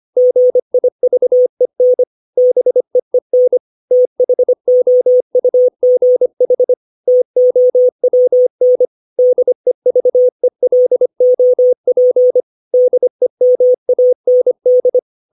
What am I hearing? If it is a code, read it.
Morse code, heard as a keyed tone, about 25 wpm, 500 Hz. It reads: GIVEN BEEN THOUGH TOWN DEVELOP DEMAND
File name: 632 words / 632 Words 25WPM